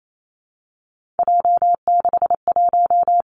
Morse code, heard as a keyed tone, about 28 wpm, 700 Hz. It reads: J61
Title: Three Challenging Letter-Number Combinations 28wpm